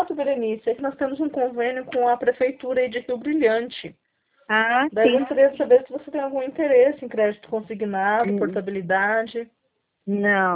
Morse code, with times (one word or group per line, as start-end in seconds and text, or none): none